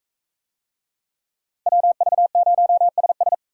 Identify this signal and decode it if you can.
Morse code, heard as a keyed tone, about 40 wpm, 700 Hz. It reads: WV0SS